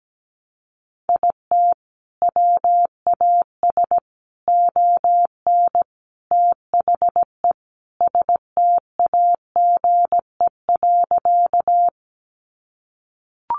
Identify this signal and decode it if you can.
Morse code, heard as a keyed tone, about 17 wpm, 700 Hz. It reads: IT WAS ON THE STAGE.
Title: Sentences from Top 1200 Words 17wpm